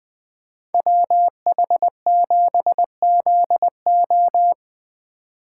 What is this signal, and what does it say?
Morse code, heard as a keyed tone, about 20 wpm, 700 Hz. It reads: WH7ZO
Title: Speed Racing - USA 2x2 Call Signs 17wpm